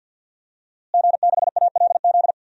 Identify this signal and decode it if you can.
Morse code, heard as a keyed tone, about 48 wpm, 700 Hz. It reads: G6ALZ
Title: DXCC-100 Call Signs - 5 Characters - Repeated 2x - 50wpm